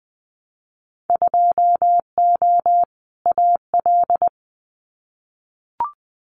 Morse code, heard as a keyed tone, about 20 wpm, 700 Hz. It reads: 2O AL
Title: Winter Field Day Exchanges Encore 20wpm